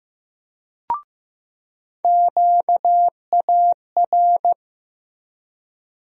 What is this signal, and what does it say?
Morse code, heard as a keyed tone, about 15 wpm, 700 Hz. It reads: QAR
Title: USA Call Sign Postfixes - 3 Letters 15wpm